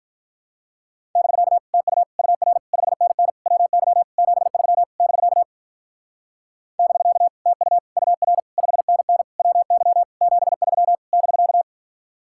Morse code, heard as a keyed tone, about 45 wpm, 700 Hz. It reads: <BK> TU UR 5NN WY 73 <BK> <BK> TU UR 5NN WY 73 <BK>